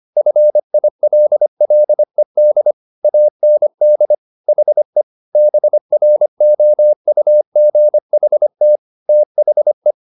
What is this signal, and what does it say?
Morse code, heard as a keyed tone, about 25 wpm, 600 Hz. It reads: FILLED AND HE BROUGHT THE